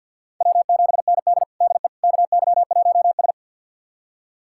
Morse code, heard as a keyed tone, about 52 wpm, 700 Hz. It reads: W7ND DE KY1S
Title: USA Callsign DE Callsign - 4 Characters - 50wpm